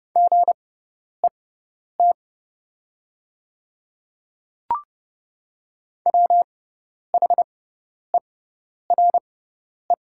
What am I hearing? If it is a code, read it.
Morse code, heard as a keyed tone, about 30 wpm, spaced as for 10 wpm, 700 Hz. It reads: GET WHERE
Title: 6x Character Spacing - Top 300 Words - 30wpm